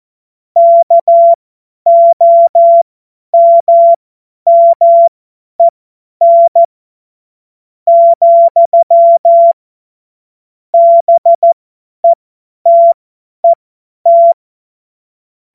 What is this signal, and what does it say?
Morse code, heard as a keyed tone, about 14 wpm, spaced as for 9 wpm, 680 Hz. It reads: KOMMEN, BETET
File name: Morsekurs / Witze / Witz 14. Februar 2023 50 F70